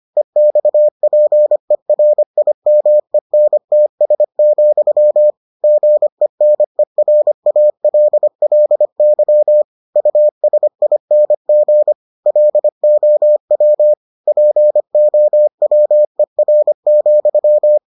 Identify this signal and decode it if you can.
Morse code, heard as a keyed tone, about 25 wpm, 600 Hz. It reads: EXPERIMENTS, GENERALLY USING LOW POWER,